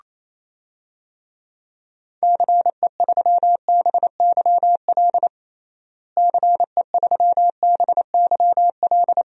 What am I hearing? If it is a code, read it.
Morse code, heard as a keyed tone, about 27 wpm, 700 Hz. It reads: CE3BYL CE3BYL
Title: DXCC-100 Call Signs - 6 Characters - Repeated 3x - 28wpm